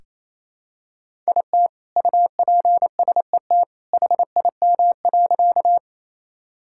Morse code, heard as a keyed tone, about 28 wpm, 700 Hz. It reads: IT UPSET HIM.
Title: Sentences from Top 1000 Words - Review Part 5 - 28wpm